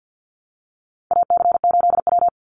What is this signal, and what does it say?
Morse code, heard as a keyed tone, about 50 wpm, 700 Hz. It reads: AC8W
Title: Call Signs Repeated 2x - 50wpm